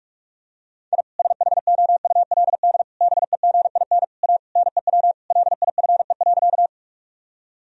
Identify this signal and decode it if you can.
Morse code, heard as a keyed tone, about 46 wpm, 700 Hz. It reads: I SHOULD BEGIN A NEW LIFE.
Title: Sentences from Top 1000 Words - Review Part 1 - 45wpm